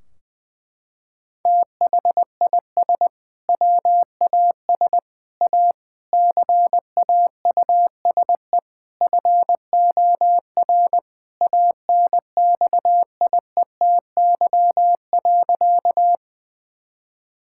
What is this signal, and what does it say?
Morse code, heard as a keyed tone, about 21 wpm, 700 Hz. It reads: THIS WAS A CAUSE FOR ANXIETY.
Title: Sentences from Top 1600 Words 20wpm